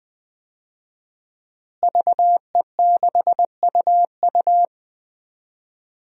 Morse code, heard as a keyed tone, about 20 wpm, 700 Hz. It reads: VE6UU
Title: DXCC-100 Call Signs - 5 Characters - Repeated 2x - 20wpm